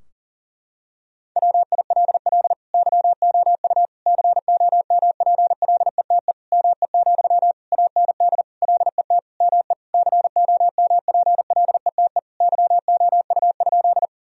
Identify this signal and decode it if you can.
Morse code, heard as a keyed tone, about 37 wpm, 700 Hz. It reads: WILL YOU COMPLETE ME, AND LET ME COMPLETE YOU?
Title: Sentences from Top 1000 Words - Review Part 2 - 40wpm